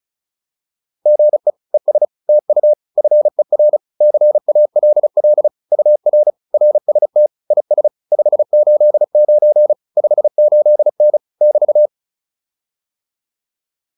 Morse code, heard as a keyed tone, about 37 wpm, 600 Hz. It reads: GE ES TU FER CALL UR RST IS 589 58N <BT>